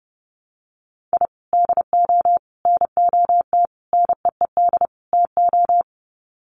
Morse code, heard as a keyed tone, about 30 wpm, 700 Hz. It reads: I DO NOT NEED TO
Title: Sets of 5 Words 30wpm